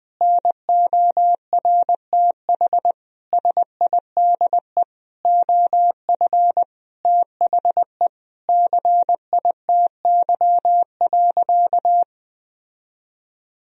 Morse code, heard as a keyed tone, about 21 wpm, 700 Hz. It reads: NORTH SIDE OF THE CITY.
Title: Sentences from Top 1000 Words - Review Part 1 - 20wpm